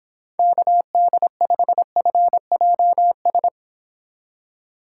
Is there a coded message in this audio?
Yes: Morse code from 0.4 to 3.5 s, heard as a keyed tone, about 26 wpm, 700 Hz. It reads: KD5FJS